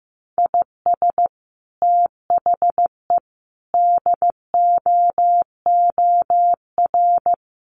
Morse code, heard as a keyed tone, about 15 wpm, 700 Hz. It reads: IS THE DOOR
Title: Sentences from Top 1300 Words 15wpm